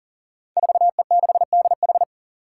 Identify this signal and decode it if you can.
Morse code, heard as a keyed tone, about 40 wpm, 700 Hz. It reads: 4E6DH
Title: Five Challenging Letter-Number Combinations 40wpm